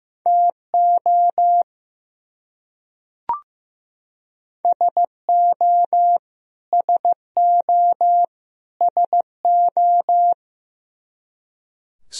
Morse code, heard as a keyed tone, about 15 wpm, 700 Hz. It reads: TO SO SO SO
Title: Top 100 Words Repeated 15wpm